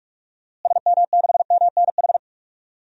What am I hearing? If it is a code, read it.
Morse code, heard as a keyed tone, about 43 wpm, 700 Hz. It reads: SM6MNH
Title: DXCC-100 Call Signs - 6 Characters - Repeated 2x - 45wpm